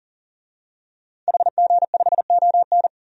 Morse code, heard as a keyed tone, about 40 wpm, 700 Hz. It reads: HG5ON
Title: DXCC-100 Call Signs - 5 Characters - 40wpm